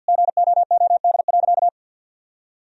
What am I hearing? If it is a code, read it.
Morse code, heard as a keyed tone, about 48 wpm, 700 Hz. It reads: GOOD.
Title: Sentences from Top 1000 Words - Review Part 1 - 50wpm